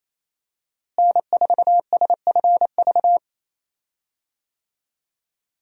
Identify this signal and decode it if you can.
Morse code, heard as a keyed tone, about 28 wpm, 700 Hz. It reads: N4SFV
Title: USA Call Signs - 5 Characters - Repeated 2x - 28wpm